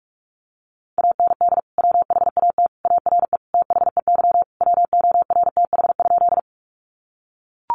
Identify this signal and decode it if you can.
Morse code, heard as a keyed tone, about 44 wpm, 700 Hz. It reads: AND WHAT ARE THEY WORTH?
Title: Sentences from Top 800 Words 45wpm